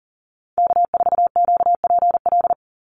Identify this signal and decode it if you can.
Morse code, heard as a keyed tone, about 40 wpm, 700 Hz. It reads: K4QPL
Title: Call Signs - 40wpm